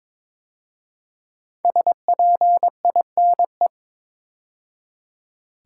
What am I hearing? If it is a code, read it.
Morse code, heard as a keyed tone, about 22 wpm, 700 Hz. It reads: SPINE
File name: 5 Letter Words - First 3 Letters Given - 22wpm